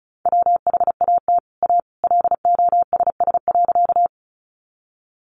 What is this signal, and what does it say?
Morse code, heard as a keyed tone, about 34 wpm, 700 Hz. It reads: WHAT A LOSS.
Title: Sentences from Top 1400 Words 35wpm